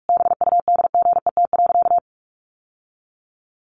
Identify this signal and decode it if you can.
Morse code, heard as a keyed tone, about 44 wpm, 700 Hz. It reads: BUDGET.
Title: Sentences from Top 1200 Words 45wpm